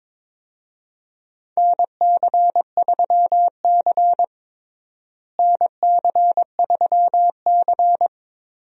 Morse code, heard as a keyed tone, about 22 wpm, 700 Hz. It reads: NC3C NC3C
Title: USA Call Signs - 4 Characters - Repeated 3x - 22wpm